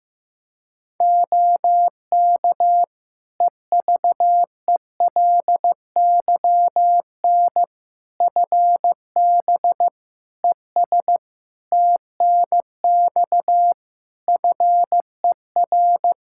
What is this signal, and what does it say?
Morse code, heard as a keyed tone, about 16 wpm, 700 Hz. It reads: OK EVELYN FB ES TNX FER
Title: Practice QSO - Exchange 2 - Info - 15wpm